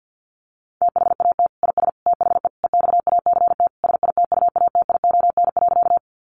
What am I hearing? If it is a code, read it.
Morse code, heard as a keyed tone, about 50 wpm, 700 Hz. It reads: THAT IS THE EXACT SITUATION.
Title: Sentences from Top 1300 Words 50wpm